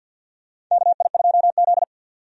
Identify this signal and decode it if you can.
Morse code, heard as a keyed tone, about 52 wpm, 700 Hz. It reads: KI2Z
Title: USA Call Signs - 4 Characters - 50wpm